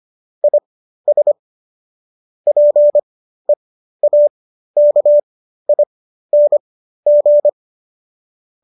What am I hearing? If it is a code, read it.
Morse code, heard as a keyed tone, about 25 wpm, spaced as for 13 wpm, 590 Hz. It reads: IS PEAKING